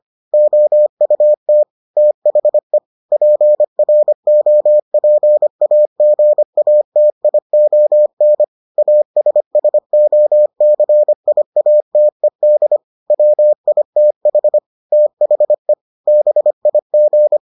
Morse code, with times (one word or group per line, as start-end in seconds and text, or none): 0.3-1.6 OUT
2.0-2.8 THE
3.1-8.4 PROPAGATION
8.8-12.8 ASSOCIATED
13.1-14.6 WITH
14.9-15.7 THE
16.1-17.4 BIG